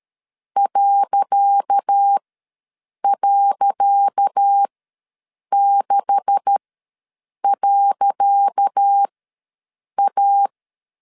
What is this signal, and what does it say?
Morse code, heard as a keyed tone, about 13 wpm, spaced as for 7 wpm, 800 Hz. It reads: ..6.A
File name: LEZIONI / 10 SIGNS / 10C DOT ws 12wpm 3548